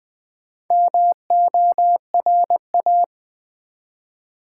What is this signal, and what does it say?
Morse code, heard as a keyed tone, about 20 wpm, 700 Hz. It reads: MORA
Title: Build A Word - 1 Letter At A Time - 5-Letter Words - 20wpm